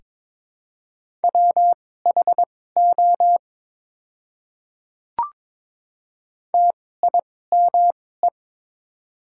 Morse code, heard as a keyed tone, about 22 wpm, spaced as for 16 wpm, 700 Hz. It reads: WHO TIME
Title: 2x Character Spacing - Top 200 Words - 22wpm